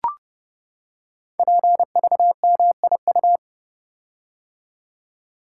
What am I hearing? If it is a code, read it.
Morse code, heard as a keyed tone, about 31 wpm, 700 Hz. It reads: PVMIU
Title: Code Groups - 5 Letter:Numbers - 30wpm